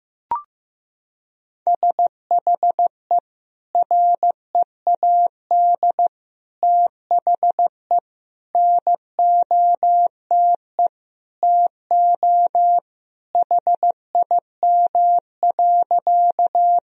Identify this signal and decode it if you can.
Morse code, heard as a keyed tone, about 15 wpm, 700 Hz. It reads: SHE READ THE NOTE TO HIM.
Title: Sentences from Top 1000 Words - Review Part 2 - 15wpm